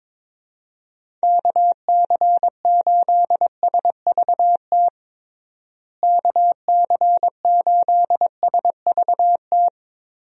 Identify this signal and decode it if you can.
Morse code, heard as a keyed tone, about 22 wpm, 700 Hz. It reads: KC8SVT KC8SVT
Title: USA Call Signs - 6 Characters - Repeated 3x - 22wpm